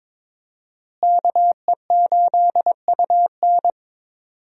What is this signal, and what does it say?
Morse code, heard as a keyed tone, about 22 wpm, 700 Hz. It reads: KE8UN
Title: USA Call Signs - 5 Characters - Repeated 2x - 22wpm